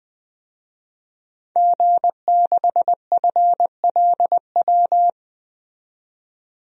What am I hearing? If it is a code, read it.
Morse code, heard as a keyed tone, about 20 wpm, 700 Hz. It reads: G6FLW
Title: DXCC-100 Call Signs - 5 Characters - 20wpm